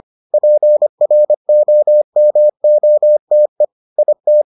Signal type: Morse code

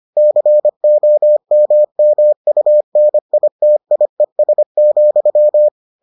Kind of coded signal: Morse code

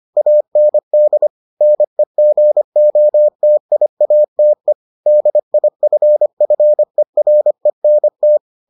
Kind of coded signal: Morse code

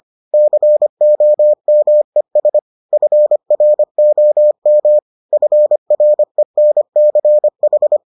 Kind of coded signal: Morse code

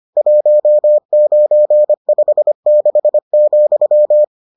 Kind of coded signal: Morse code

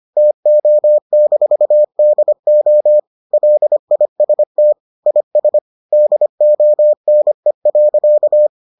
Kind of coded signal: Morse code

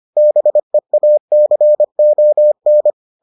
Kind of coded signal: Morse code

